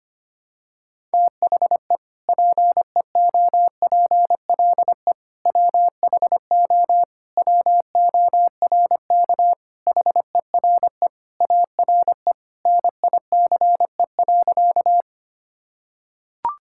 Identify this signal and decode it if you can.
Morse code, heard as a keyed tone, about 25 wpm, 700 Hz. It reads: THE PEOPLE WHO WORK HERE ARE NICE.
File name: practice / 25wpm / Sentences with 3-Word Phrases - 25wpm